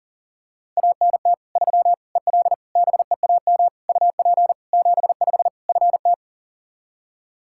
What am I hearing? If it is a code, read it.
Morse code, heard as a keyed tone, about 40 wpm, 700 Hz. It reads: ANT 3 EL BEAM UP 75 FT
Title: Practice QSO - Exchange 2 - Info - 40wpm